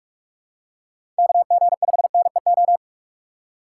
Morse code, heard as a keyed tone, about 46 wpm, 700 Hz. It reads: KG5NEO